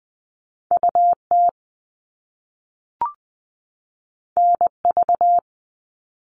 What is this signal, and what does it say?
Morse code, heard as a keyed tone, about 20 wpm, 700 Hz. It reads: UT NV
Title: US States 20wpm